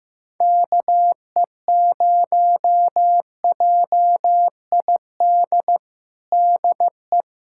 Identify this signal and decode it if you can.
Morse code, heard as a keyed tone, about 15 wpm, 700 Hz. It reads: KE0JID DE
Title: USA Callsign DE Callsign - 6 Characters - 15wpm